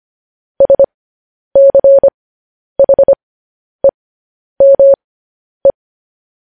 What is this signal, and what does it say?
Morse code, heard as a keyed tone, about 25 wpm, spaced as for 10 wpm, 550 Hz. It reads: SCHEME